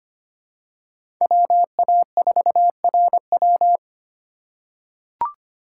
Morse code, heard as a keyed tone, about 25 wpm, 700 Hz. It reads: WA4RW
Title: USA Call Signs - 5 Characters - 4x Character Spacing - 25wpm